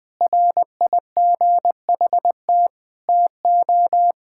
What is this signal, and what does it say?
Morse code, heard as a keyed tone, about 20 wpm, 700 Hz. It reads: RIGHT TO
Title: Sentences with 7-Word Phrases - 20wpm